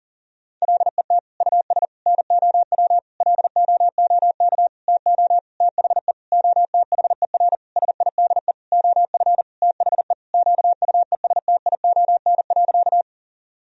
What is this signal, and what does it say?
Morse code, heard as a keyed tone, about 40 wpm, 700 Hz. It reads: LET US NOW LOOK TO THE OTHER SIDE OF THE QUESTION.
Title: Sentences from Top 1000 Words - Review Part 1 - 40wpm